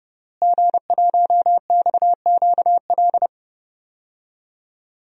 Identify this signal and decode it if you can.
Morse code, heard as a keyed tone, about 32 wpm, 700 Hz. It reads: G1XQL
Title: DXCC-100 Call Signs - 5 Characters - Repeated 3x - 30wpm